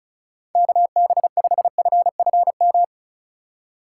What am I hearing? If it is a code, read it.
Morse code, heard as a keyed tone, about 36 wpm, 700 Hz. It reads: KB5FFM
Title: USA Call Signs - 6 Characters - 3x Character Spacing - 35wpm